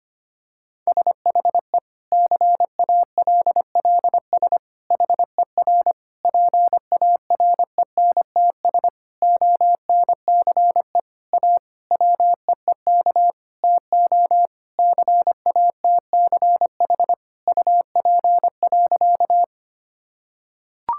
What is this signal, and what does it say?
Morse code, heard as a keyed tone, about 25 wpm, 700 Hz. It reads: SHE CALLS HER PARENTS ONCE A WEEK TO CATCH UP.
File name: practice / 25wpm / Difficult Sentences with Hints - 25wpm